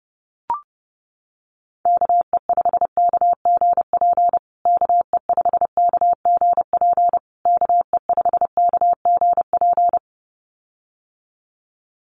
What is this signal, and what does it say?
Morse code, heard as a keyed tone, about 30 wpm, 700 Hz. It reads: KE5KGP KE5KGP KE5KGP
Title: USA Call Signs Repeated 3x 30wpm